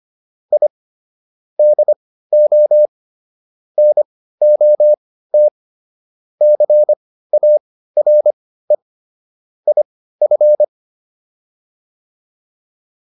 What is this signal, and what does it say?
Morse code, heard as a keyed tone, about 25 wpm, spaced as for 15 wpm, 600 Hz. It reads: I DO NOT CARE IF